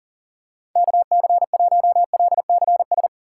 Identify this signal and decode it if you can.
Morse code, heard as a keyed tone, about 40 wpm, 700 Hz. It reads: KC1LCS
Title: USA Call Signs - 6 Characters - 3x Character Spacing - 40wpm